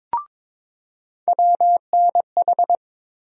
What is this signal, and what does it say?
Morse code, heard as a keyed tone, about 22 wpm, 700 Hz. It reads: WNH
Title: USA Call Sign Postfixes - 3 Letters 22wpm